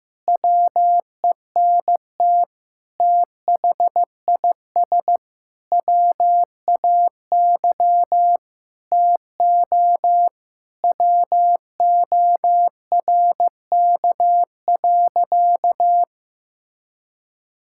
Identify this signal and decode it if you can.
Morse code, heard as a keyed tone, about 15 wpm, 700 Hz. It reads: WENT THIS WAY TO WORK.
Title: Sentences from Top 1000 Words - Review Part 5 - 15wpm